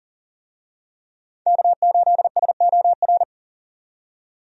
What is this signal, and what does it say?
Morse code, heard as a keyed tone, about 40 wpm, 700 Hz. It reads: K8SOR
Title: USA Call Signs - 5 Characters - 40wpm